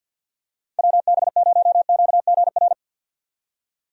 Morse code, heard as a keyed tone, about 51 wpm, 700 Hz. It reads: WB0QGR
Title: USA Call Signs - 6 Characters - 50wpm